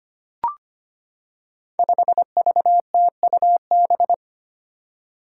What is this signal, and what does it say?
Morse code, heard as a keyed tone, about 25 wpm, 700 Hz. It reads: 5VTUB
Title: Five Challenging Letter-Number Combinations 25wpm